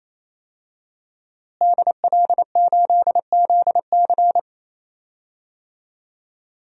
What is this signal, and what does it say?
Morse code, heard as a keyed tone, about 28 wpm, 700 Hz. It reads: DL8ZC
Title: DXCC-100 Call Signs - 5 Characters - Repeated 3x - 28wpm